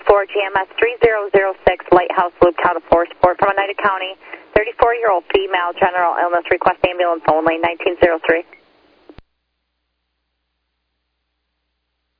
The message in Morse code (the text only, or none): none